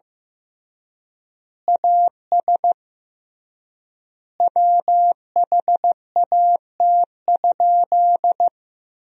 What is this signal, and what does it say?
Morse code, heard as a keyed tone, about 16 wpm, 700 Hz. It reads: AS WHAT?
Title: 3x Word-Spacing - Sentences from Top 200 Words - 15wpm